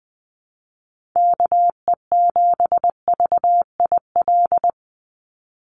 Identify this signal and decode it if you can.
Morse code, heard as a keyed tone, about 20 wpm, 700 Hz. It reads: KE7VIL
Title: USA Call Signs 20wpm